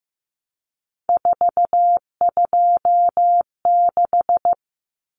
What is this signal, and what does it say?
Morse code, heard as a keyed tone, about 15 wpm, 700 Hz. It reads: 426
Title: Common Cut Numbers - 3-Digit 15wpm